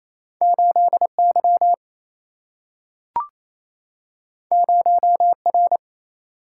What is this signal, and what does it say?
Morse code, heard as a keyed tone, about 28 wpm, 700 Hz. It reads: 8Y 0R
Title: Single-Letter Single-Number 28wpm